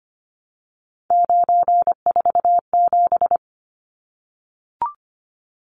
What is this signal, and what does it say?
Morse code, heard as a keyed tone, about 25 wpm, 700 Hz. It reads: 947
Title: Numbers - 3-Digit 25wpm